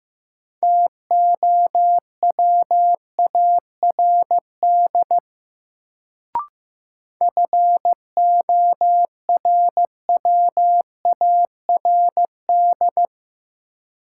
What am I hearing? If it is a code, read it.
Morse code, heard as a keyed tone, about 15 wpm, 700 Hz. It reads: TOWARD FORWARD
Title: Sentences from Top 1300 Words - New Word Drill Alt - 15wpm